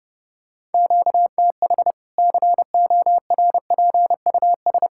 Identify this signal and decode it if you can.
Morse code, heard as a keyed tone, about 32 wpm, 700 Hz. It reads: QTH CORPUS
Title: K1USN SST QSO Practice 30wpm